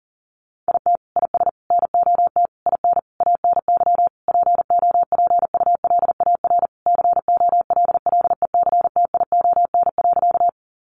Morse code, heard as a keyed tone, about 40 wpm, 700 Hz. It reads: IT IS NOT IN ANY POPULAR COLLECTION.
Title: Sentences from Top 1200 Words 40wpm